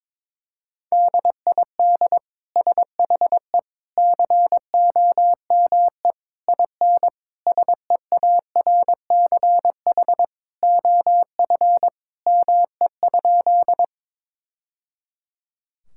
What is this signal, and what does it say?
Morse code, heard as a keyed tone, about 22 wpm, 700 Hz. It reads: DID SHE COME IN SEARCH OF ME?
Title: Sentences from Top 1000 Words - Review Part 5 - 22wpm